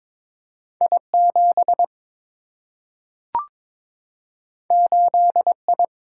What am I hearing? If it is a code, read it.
Morse code, heard as a keyed tone, about 22 wpm, 700 Hz. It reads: I 7 8 I
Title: Single-Letter Single-Number 22wpm